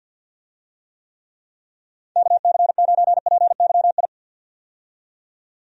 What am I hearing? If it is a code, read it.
Morse code, heard as a keyed tone, about 47 wpm, 700 Hz. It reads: KC9PYI
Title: USA Call Signs - 6 Characters - 50wpm